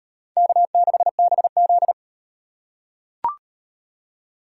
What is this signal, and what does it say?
Morse code, heard as a keyed tone, about 38 wpm, 700 Hz. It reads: K6BZ